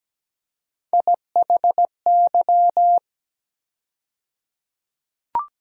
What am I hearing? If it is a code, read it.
Morse code, heard as a keyed tone, about 17 wpm, 700 Hz. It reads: IHY
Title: USA Call Sign Postfixes - 3 Letters 17wpm